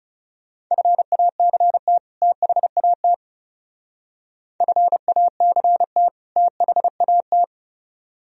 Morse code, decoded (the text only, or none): FACT THAT FACT THAT